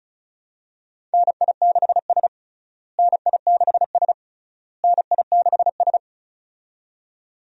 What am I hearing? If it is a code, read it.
Morse code, heard as a keyed tone, about 36 wpm, 700 Hz. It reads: NI6S NI6S NI6S
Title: USA Call Signs - 4 Characters - Repeated 3x - 35wpm